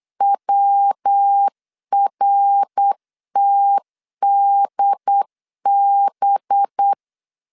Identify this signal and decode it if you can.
Morse code, heard as a keyed tone, about 8 wpm, 800 Hz. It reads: WRTDB
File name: LEZIONI / 6 PKX0 / 6 P 8wpm 3025